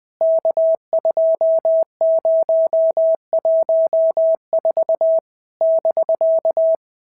Morse code, heard as a keyed tone, about 20 wpm, 650 Hz. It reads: K2014 <BK>
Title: POTA QSO Practice - 20wpm